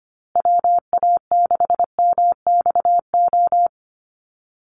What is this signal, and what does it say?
Morse code, heard as a keyed tone, about 25 wpm, 700 Hz. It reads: WA6MXO
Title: USA Call Signs Repeated 2x 25wpm